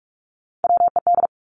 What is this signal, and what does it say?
Morse code, heard as a keyed tone, about 46 wpm, 700 Hz. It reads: WED
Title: Three Challenging Letter-Number Combinations 45wpm